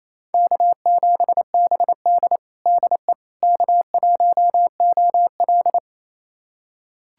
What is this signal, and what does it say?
Morse code, heard as a keyed tone, about 28 wpm, 700 Hz. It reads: K7BD DE K1OL
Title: USA Callsign DE Callsign - 4 Characters - 28wpm